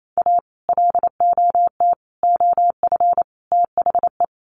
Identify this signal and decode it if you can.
Morse code, heard as a keyed tone, about 28 wpm, 700 Hz. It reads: A LOT OF THE
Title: Sets of 4 Words 28wpm